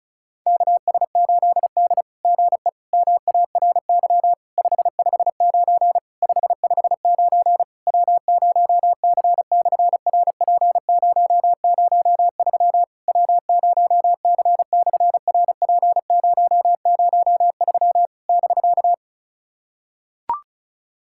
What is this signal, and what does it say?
Morse code, heard as a keyed tone, about 33 wpm, 700 Hz. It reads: KS8D GE MARY 559 559 W0C/RP003 W0C/RP003 <BK>
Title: SOTA QSO Practice 35wpm